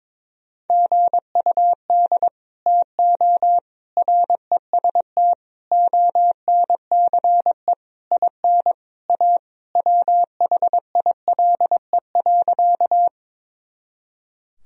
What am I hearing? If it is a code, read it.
Morse code, heard as a keyed tone, about 22 wpm, 700 Hz. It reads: GUD TO REST ONCE IN A WHILE.
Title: Sentences from Top 500 Words and Abbreviations 22wpm